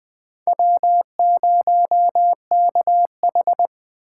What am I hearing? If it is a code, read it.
Morse code, heard as a keyed tone, about 20 wpm, 700 Hz. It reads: W0KH
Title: USA Call Signs - 4 Characters - 4x Character Spacing - 20wpm